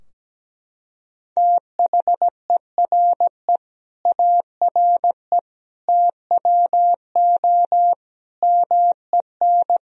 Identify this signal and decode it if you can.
Morse code, heard as a keyed tone, about 17 wpm, 700 Hz. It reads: THERE ARE TWO MEN
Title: Sentences from Top 1000 Words - Review Part 3 - 17wpm